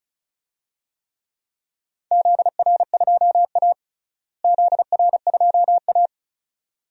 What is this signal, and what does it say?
Morse code, heard as a keyed tone, about 36 wpm, 700 Hz. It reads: ZR2A ZR2A
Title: DXCC-100 Call Signs - 4 Characters - Repeated 2x - 35wpm